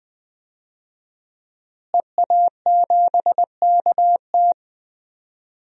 Morse code, heard as a keyed tone, about 20 wpm, 700 Hz. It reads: EA7KT
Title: DXCC-100 Call Signs - 5 Characters - Repeated 3x - 20wpm